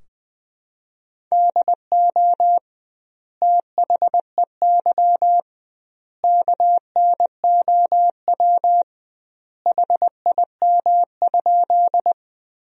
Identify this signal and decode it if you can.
Morse code, heard as a keyed tone, about 20 wpm, 700 Hz. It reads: DO THEY KNOW HIM?
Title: 2x Word-Spacing - Sentences from Top 100 Words - 20wpm